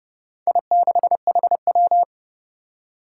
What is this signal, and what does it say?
Morse code, heard as a keyed tone, about 30 wpm, 700 Hz. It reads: I6HW
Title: Four Challenging Letter-Number Combinations 30wpm